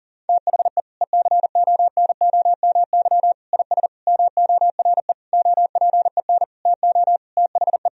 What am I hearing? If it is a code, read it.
Morse code, heard as a keyed tone, about 40 wpm, 700 Hz. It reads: THE ECONOMY IS MORE OPEN TO THE
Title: Sentences from Top 1000 Words - Review Part 3 - 40wpm